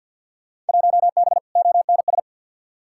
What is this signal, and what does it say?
Morse code, heard as a keyed tone, about 48 wpm, 700 Hz. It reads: 1B ONS
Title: Field Day Exchanges Encore - 50wpm